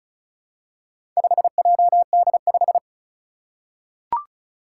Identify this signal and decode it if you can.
Morse code, heard as a keyed tone, about 34 wpm, 700 Hz. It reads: 5JD5
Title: Four Challenging Letter-Number Combinations 35wpm